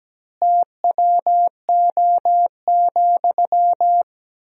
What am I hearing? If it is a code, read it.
Morse code, heard as a keyed tone, about 17 wpm, 700 Hz. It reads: TWO,
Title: 2x Word-Spacing - Sentences from Top 200 Words - 17wpm